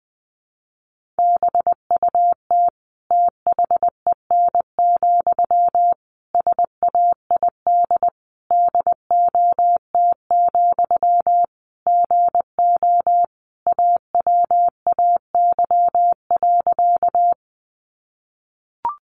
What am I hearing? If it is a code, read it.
Morse code, heard as a keyed tone, about 20 wpm, 700 Hz. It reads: BUT THEN, SAID DOT, GO AWAY.
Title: Sentences from Top 1500 Words 20wpm